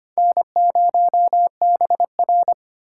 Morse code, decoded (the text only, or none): N0BR